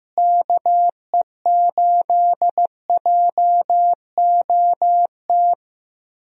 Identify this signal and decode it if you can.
Morse code, heard as a keyed tone, about 15 wpm, 700 Hz. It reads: KE8JOT
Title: USA Call Signs - 6 Characters - Repeated 3x - 15wpm